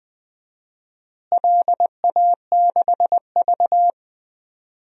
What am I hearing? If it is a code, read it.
Morse code, heard as a keyed tone, about 20 wpm, 700 Hz. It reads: LA6V
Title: DXCC-100 Call Signs - 4 Characters - Repeated 2x - 20wpm